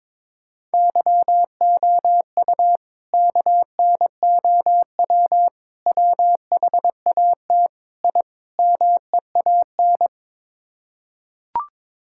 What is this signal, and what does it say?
Morse code, heard as a keyed tone, about 22 wpm, 700 Hz. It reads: YOU KNOW WHAT I MEAN
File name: practice / 22wpm / Build A Phrase - 5-Word Phrases - 22wpm